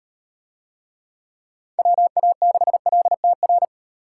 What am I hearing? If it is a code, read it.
Morse code, heard as a keyed tone, about 36 wpm, 700 Hz. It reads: WA6LTR